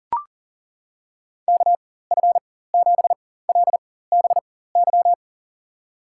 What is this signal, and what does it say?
Morse code, heard as a keyed tone, about 40 wpm, spaced as for 19 wpm, 700 Hz. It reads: KF7LBY
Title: USA Call Signs - 4x Character Spacing - 40wpm